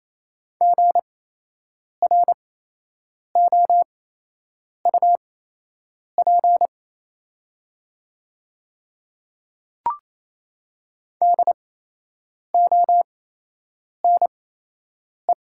Morse code, heard as a keyed tone, about 27 wpm, spaced as for 8 wpm, 700 Hz. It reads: GROUP DONE
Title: 8x Character Spacing - Top 500 Words - 28wpm